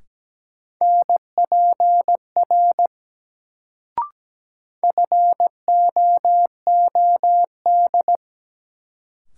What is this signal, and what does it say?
Morse code, heard as a keyed tone, about 17 wpm, 700 Hz. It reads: NPR FOOD
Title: Sets of 5 Words - New Word Drill - 17wpm